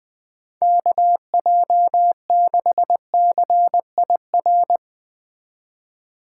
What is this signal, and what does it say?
Morse code, heard as a keyed tone, about 20 wpm, 700 Hz. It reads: KJ6CIR
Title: Speed Racing - USA 2x3 Call Signs 17wpm